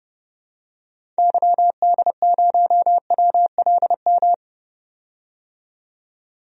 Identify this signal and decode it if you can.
Morse code, heard as a keyed tone, about 30 wpm, 700 Hz. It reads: YD0WLM
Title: DXCC-100 Call Signs - 6 Characters - 30wpm